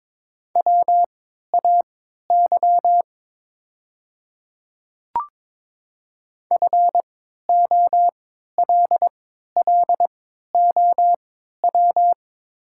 Morse code, heard as a keyed tone, about 22 wpm, spaced as for 12 wpm, 700 Hz. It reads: WAY FOLLOW
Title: 3x Character Spacing - Top 200 Words - 22wpm